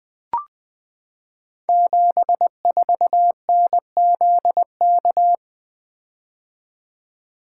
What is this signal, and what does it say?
Morse code, heard as a keyed tone, about 20 wpm, 700 Hz. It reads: 74NZK
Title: Code Groups - 5 Letter:Numbers - 20wpm